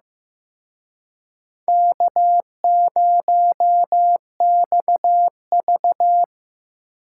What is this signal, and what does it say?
Morse code, heard as a keyed tone, about 18 wpm, 700 Hz. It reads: K0XV
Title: Speed Racing - USA 1x2 Call Signs 15wpm